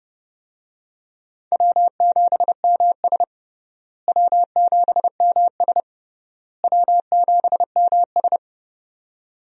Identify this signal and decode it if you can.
Morse code, heard as a keyed tone, about 30 wpm, 700 Hz. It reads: W7MS W7MS W7MS
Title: USA Call Signs - 4 Characters - Repeated 3x - 30wpm